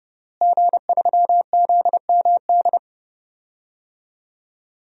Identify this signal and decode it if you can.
Morse code, heard as a keyed tone, about 32 wpm, 700 Hz. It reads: G3ZMD
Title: DXCC-100 Call Signs - 5 Characters - Repeated 2x - 30wpm